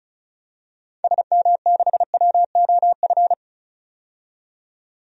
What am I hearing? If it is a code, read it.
Morse code, heard as a keyed tone, about 34 wpm, 700 Hz. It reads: SM6WOF
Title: DXCC-100 Call Signs - 6 Characters - 35wpm